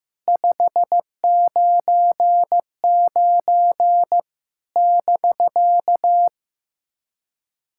Morse code, heard as a keyed tone, about 15 wpm, 700 Hz. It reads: 599 <BK>
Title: SOTA QSO Practice 15wpm